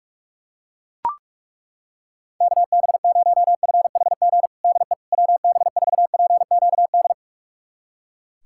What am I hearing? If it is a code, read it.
Morse code, heard as a keyed tone, about 41 wpm, 700 Hz. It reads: KB0FHG DE WB4PQD
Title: USA Callsign DE Callsign - 6 Characters - 45wpm